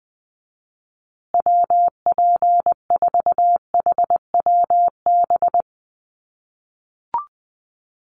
Morse code, heard as a keyed tone, about 20 wpm, 700 Hz. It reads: WP4HWB
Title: USA Call Signs Repeated 2x 20wpm